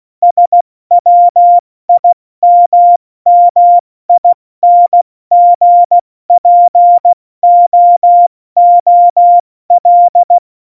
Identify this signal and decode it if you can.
Morse code, heard as a keyed tone, about 16 wpm, 700 Hz. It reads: SWIMMINGPOOL
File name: Morsekurs / Witze / Witz 17. November 2020, 70 F80